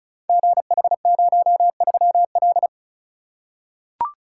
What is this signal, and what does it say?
Morse code, heard as a keyed tone, about 36 wpm, 690 Hz. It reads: GH03L
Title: Code Groups - 5 Letter:Numbers - 35wpm